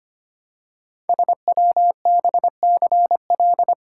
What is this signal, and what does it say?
Morse code, heard as a keyed tone, about 25 wpm, 700 Hz. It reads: SWBCL
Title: Code Groups - 5 Letters - 25wpm